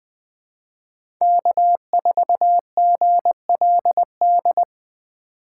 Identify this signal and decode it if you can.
Morse code, heard as a keyed tone, about 20 wpm, 700 Hz. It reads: K4GLD
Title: Speed Racing - USA 1x3 Call Signs 17wpm